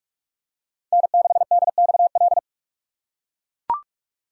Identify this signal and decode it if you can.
Morse code, heard as a keyed tone, about 42 wpm, 700 Hz. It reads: N6DXL
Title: USA Call Signs - 5 Characters - 2x Character Spacing - 45wpm